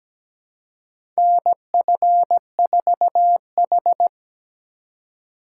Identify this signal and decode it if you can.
Morse code, heard as a keyed tone, about 17 wpm, 700 Hz. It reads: NF4H
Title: USA Call Signs - 4 Characters - 4x Character Spacing - 17wpm